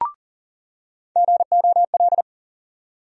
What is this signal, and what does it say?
Morse code, heard as a keyed tone, about 40 wpm, 700 Hz. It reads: GOL
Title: USA Call Sign Postfixes - 3 Letters 40wpm